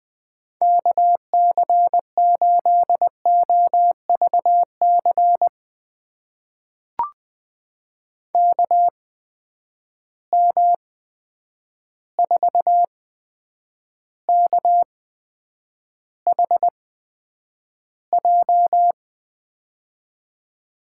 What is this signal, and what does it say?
Morse code, heard as a keyed tone, about 20 wpm, spaced as for 9 wpm, 700 Hz. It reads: KC8OVC KM4KHJ
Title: USA Call Signs - 6 Characters - 8x Character Spacing - 20wpm